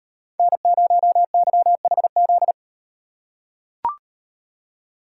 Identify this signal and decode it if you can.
Morse code, heard as a keyed tone, about 36 wpm, 700 Hz. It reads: N0YHZ